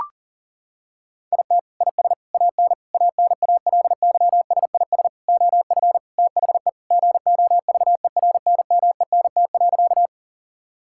Morse code, heard as a keyed tone, about 40 wpm, 700 Hz. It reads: IT IS AN ANALYSIS OF THE GOVERNMENT.
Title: Sentences from Top 1000 Words - Review Part 5 - 40wpm